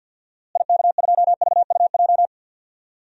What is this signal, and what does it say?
Morse code, heard as a keyed tone, about 52 wpm, 700 Hz. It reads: IK2VUJ
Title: DXCC-100 Call Signs - 6 Characters - Repeated 3x - 50wpm